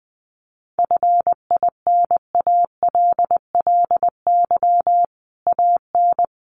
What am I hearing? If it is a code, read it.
Morse code, heard as a keyed tone, about 19 wpm, 700 Hz. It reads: FINALLY AN
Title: Sentences from Top 1000 Words 20wpm